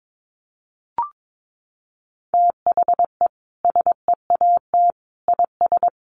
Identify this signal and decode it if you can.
Morse code, heard as a keyed tone, about 22 wpm, 700 Hz. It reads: THE SEAT IS
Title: Sentences from Top 1500 Words 22wpm